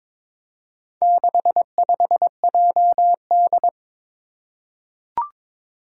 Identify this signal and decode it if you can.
Morse code, heard as a keyed tone, about 22 wpm, 700 Hz. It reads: 65JD
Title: Four Challenging Letter-Number Combinations 22wpm